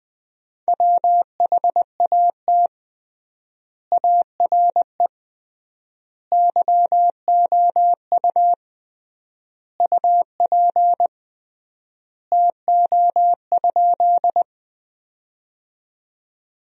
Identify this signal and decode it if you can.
Morse code, heard as a keyed tone, about 20 wpm, 700 Hz. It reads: WHAT ARE YOU UP TO?
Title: 3x Word-Spacing - Sentences from Top 100 Words - 20wpm